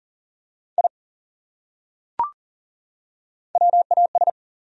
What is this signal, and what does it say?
Morse code, heard as a keyed tone, about 40 wpm, 700 Hz. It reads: I WAS
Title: Top 100 Words 40wpm